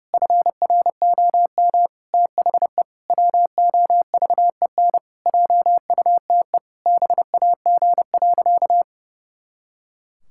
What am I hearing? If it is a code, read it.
Morse code, heard as a keyed tone, about 30 wpm, 700 Hz. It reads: FROM THE WOVEN JUTE BAG.
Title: Pangram Challenge v2 - 30wpm